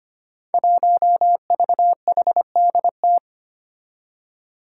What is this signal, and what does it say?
Morse code, heard as a keyed tone, about 25 wpm, 700 Hz. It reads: 1VHDT
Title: Five Challenging Letter-Number Combinations 25wpm